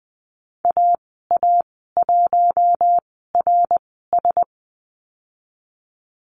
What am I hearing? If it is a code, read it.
Morse code, heard as a keyed tone, about 20 wpm, spaced as for 15 wpm, 700 Hz. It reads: AA1RS
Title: 076 Lesson 13g - Letter L - Call Signs 20wpm